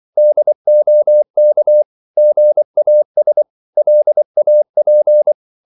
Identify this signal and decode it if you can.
Morse code, heard as a keyed tone, about 24 wpm, 600 Hz. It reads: DOK GAS LAP
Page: Korte danske ord 24wpm